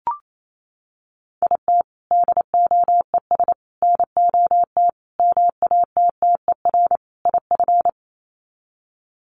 Morse code, heard as a keyed tone, about 28 wpm, 700 Hz. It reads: IT DOES NOT MATTER IF
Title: Sets of 5 Words Encore 28wpm